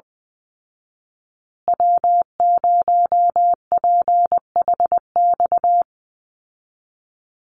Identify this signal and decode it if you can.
Morse code, heard as a keyed tone, about 20 wpm, 700 Hz. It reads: W0PHX
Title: Call Signs Repeated 3x - 20wpm